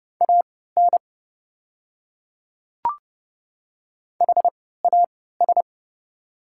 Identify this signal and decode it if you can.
Morse code, heard as a keyed tone, about 30 wpm, spaced as for 17 wpm, 690 Hz. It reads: AN HAS
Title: 3x Character Spacing - Top 200 Words - 30wpm